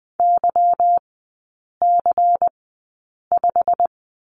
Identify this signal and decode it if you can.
Morse code, heard as a keyed tone, about 20 wpm, spaced as for 8 wpm, 690 Hz. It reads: YC5
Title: 145 Lesson 24e - Letter y - Three Characters 20wpm